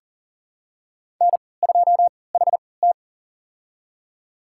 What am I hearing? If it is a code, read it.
Morse code, heard as a keyed tone, about 40 wpm, spaced as for 23 wpm, 700 Hz. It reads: N2HT
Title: USA Call Signs - 4 Characters - 3x Character Spacing - 40wpm